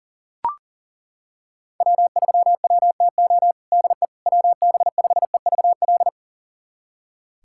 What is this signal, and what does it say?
Morse code, heard as a keyed tone, about 40 wpm, 700 Hz. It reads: W3WTO DE WB5EVL
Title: USA Callsign DE Callsign 40wpm